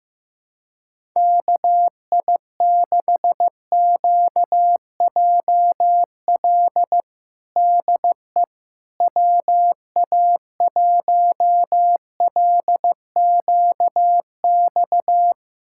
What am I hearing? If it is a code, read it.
Morse code, heard as a keyed tone, about 15 wpm, 700 Hz. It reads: KI6QJL DE WA1LQX
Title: USA Callsign DE Callsign - 6 Characters - 15wpm